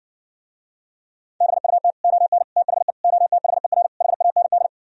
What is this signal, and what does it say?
Morse code, heard as a keyed tone, about 69 wpm, 700 Hz. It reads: BUT ON THE OTHER HAND